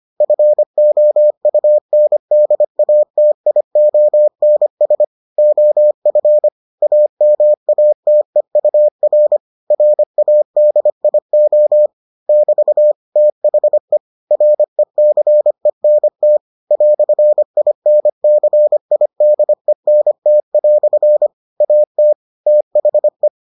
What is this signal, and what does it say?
Morse code, heard as a keyed tone, about 25 wpm, 600 Hz. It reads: FOUNDATIONS OF AMATEUR RADIO <BT> THE RECENT "INCIDENT" AT THE